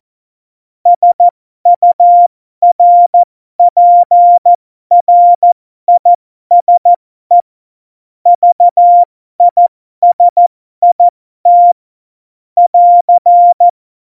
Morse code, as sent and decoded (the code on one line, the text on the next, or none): ... ..- .-. .--. .-. .. ... / ...- .. ... .. - / .-.-.
SURPRISE VISIT <AR>